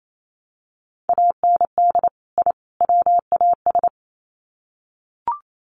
Morse code, heard as a keyed tone, about 28 wpm, 700 Hz. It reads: AND I WAS
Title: Sets of 3 Words 28wpm